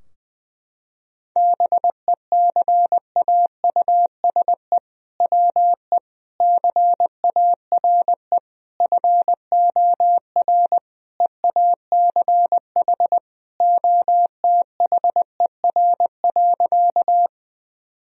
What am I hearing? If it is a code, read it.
Morse code, heard as a keyed tone, about 21 wpm, 700 Hz. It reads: BECAUSE WE CARE FOR EACH OTHER.
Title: Sentences from Top 1000 Words - Review Part 3 - 20wpm